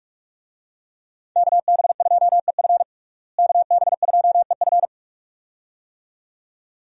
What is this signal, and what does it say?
Morse code, heard as a keyed tone, about 46 wpm, 700 Hz. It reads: KB2EF KB2EF
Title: USA Call Signs - 5 Characters - Repeated 2x - 45wpm